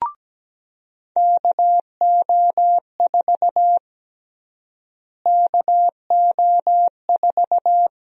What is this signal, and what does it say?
Morse code, heard as a keyed tone, about 17 wpm, 700 Hz. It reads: KO4 KO4
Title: USA Call Sign Prefixes - Repeated 2x - 17wpm